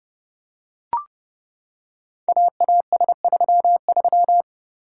Morse code, heard as a keyed tone, about 30 wpm, 700 Hz. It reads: AAS33
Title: Code Groups - 5 Letter:Numbers - 30wpm